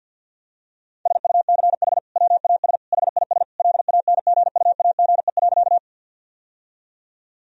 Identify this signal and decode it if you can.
Morse code, heard as a keyed tone, about 51 wpm, 700 Hz. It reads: SUCH WAS HIS LANGUAGE.
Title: Sentences from Top 1000 Words - Review Part 3 - 50wpm